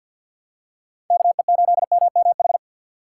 Morse code, heard as a keyed tone, about 52 wpm, 700 Hz. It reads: KE8MMH
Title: USA Call Signs - 6 Characters - 6x Character Spacing - 50wpm